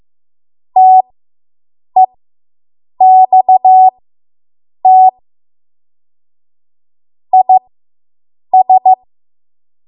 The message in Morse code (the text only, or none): TEXT IS